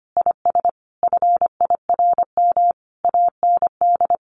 Morse code, heard as a keyed tone, about 25 wpm, 700 Hz. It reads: IS FIRM AND
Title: Sentences from Top 1500 Words 25wpm